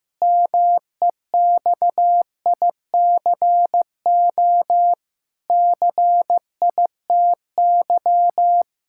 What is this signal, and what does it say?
Morse code, heard as a keyed tone, about 15 wpm, 700 Hz. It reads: MEXICO CITY
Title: Pangram Challenge v2 - 15wpm